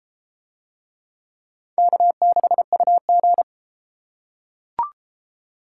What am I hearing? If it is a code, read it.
Morse code, heard as a keyed tone, about 29 wpm, 700 Hz. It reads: K6UG